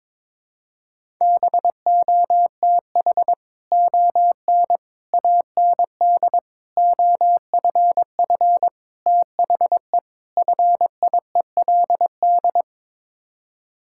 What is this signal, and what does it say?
Morse code, heard as a keyed tone, about 22 wpm, 700 Hz. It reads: BOTH ON AND OFF THE FIELD
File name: practice / 22wpm / Sets of 6 Words - 22wpm